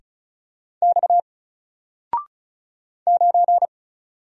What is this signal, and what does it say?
Morse code, heard as a keyed tone, about 35 wpm, 700 Hz. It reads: X 9
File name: Single-Letter-Number-35wpm